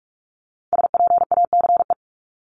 Morse code, heard as a keyed tone, about 46 wpm, 700 Hz. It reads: SPACE
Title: Multi-letter Phonemes Repeated 3x 45wpm